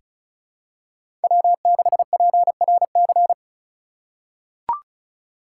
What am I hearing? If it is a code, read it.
Morse code, heard as a keyed tone, about 36 wpm, 700 Hz. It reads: W6PRC
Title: USA Call Signs - 5 Characters - 3x Character Spacing - 35wpm